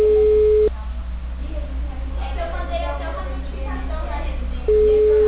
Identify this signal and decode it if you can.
Ringback tone (Europe, 425 Hz), with speech.